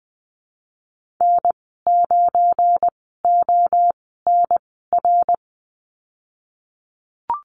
Morse code, heard as a keyed tone, about 19 wpm, spaced as for 14 wpm, 700 Hz. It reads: N9ONR
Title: 231 Lesson 37g - Letter x - Call Signs 20wpm